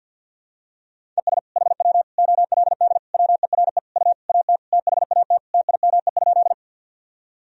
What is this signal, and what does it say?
Morse code, heard as a keyed tone, about 46 wpm, 700 Hz. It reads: ES HW OLD WERE U AT THAT TIME?
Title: Sentences from Top 200 Words and Abbreviations 50wpm